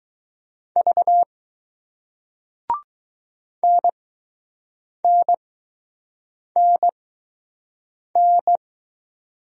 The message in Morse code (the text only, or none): V NNNN